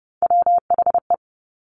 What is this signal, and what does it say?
Morse code, heard as a keyed tone, about 30 wpm, 700 Hz. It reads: WHE
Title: Top Three Letter Combinations in Words 30wpm